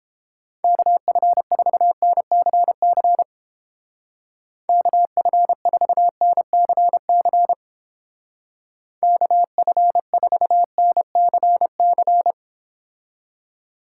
Morse code, heard as a keyed tone, about 31 wpm, 700 Hz. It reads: KF4NCC KF4NCC KF4NCC